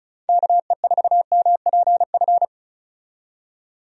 Morse code, heard as a keyed tone, about 34 wpm, 700 Hz. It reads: KE4MPF
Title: USA Call Signs - 6 Characters - 35wpm